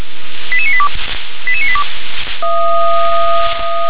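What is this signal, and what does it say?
Five-tone selective call (ZVEI), heard as quick runs of five tones, tones of about 70 ms. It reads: 88982 88982
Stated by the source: ZVEI five-tone selective call received with an SDR.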